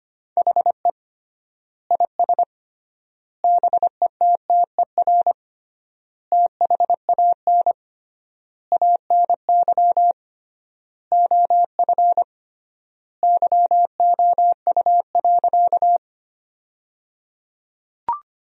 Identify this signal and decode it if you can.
Morse code, heard as a keyed tone, about 25 wpm, 700 Hz. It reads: HE IS BETTER THAN ANY OF YOU.
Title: 3x Word-Spacing - Sentences from Top 400 Words - 25wpm